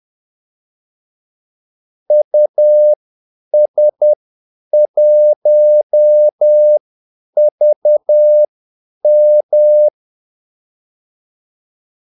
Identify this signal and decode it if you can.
Morse code, heard as a keyed tone, about 10 wpm, spaced as for 8 wpm, 600 Hz. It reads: US1VM
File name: Callsigns-10-8wpm